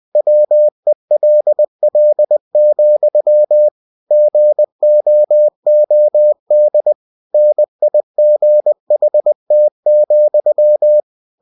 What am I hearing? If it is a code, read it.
Morse code, heard as a keyed tone, about 20 wpm, 600 Hz. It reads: WELL, GOOD NIGHT,
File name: cw-books / winnie-the-pooh / winnie-20wpm-page-0038